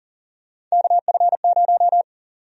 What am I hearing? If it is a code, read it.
Morse code, heard as a keyed tone, about 40 wpm, 700 Hz. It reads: KF0
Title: USA Call Sign Prefixes - 40wpm